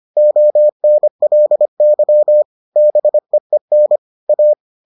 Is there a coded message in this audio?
Yes: Morse code from 0.2 to 4.5 s, heard as a keyed tone, about 25 wpm, 600 Hz. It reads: ONLY BEEN A